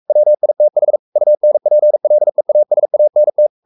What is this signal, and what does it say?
Morse code, heard as a keyed tone, about 44 wpm, 600 Hz. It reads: WITH UNPLEASANT
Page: Romanen Never af Ken Follet på engelsk.